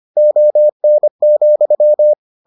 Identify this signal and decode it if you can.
Morse code, heard as a keyed tone, about 25 wpm, 600 Hz. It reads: ON,